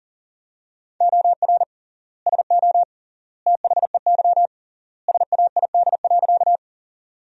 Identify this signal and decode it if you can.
Morse code, heard as a keyed tone, about 37 wpm, 700 Hz. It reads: OR SO THEY SAID.
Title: 3x Word-Spacing - Sentences from Top 100 Words - 40wpm